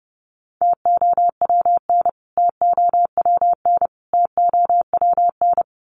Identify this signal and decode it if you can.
Morse code, heard as a keyed tone, about 29 wpm, 700 Hz. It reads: TOWN TOWN TOWN
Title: Top 500 Words Repeated 30wpm